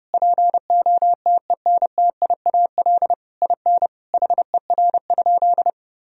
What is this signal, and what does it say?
Morse code, heard as a keyed tone, about 30 wpm, 700 Hz. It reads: POTENTIAL IN HER?
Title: Sentences from Top 1000 Words - Review Part 4 - 30wpm